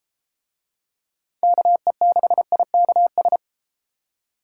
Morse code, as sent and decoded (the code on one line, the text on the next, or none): -.- . -.... .. -.- ...
KE6IKS